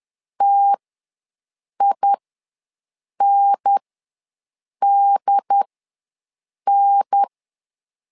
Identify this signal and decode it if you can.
Morse code, heard as a keyed tone, about 10 wpm, spaced as for 6 wpm, 800 Hz. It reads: TINDN
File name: LEZIONI / 2 UHDN / 2 UHDN ws 10wpm 2044